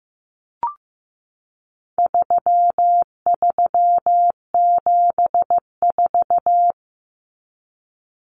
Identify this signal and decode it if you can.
Morse code, heard as a keyed tone, about 15 wpm, 700 Hz. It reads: 3374
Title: Numbers - 4-Digit 15wpm